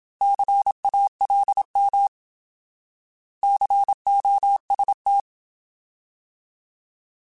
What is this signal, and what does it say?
Morse code, heard as a keyed tone, about 27 wpm, 800 Hz. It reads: CALM COST
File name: Alphabetical / Alphabetical mp3s / Cs @25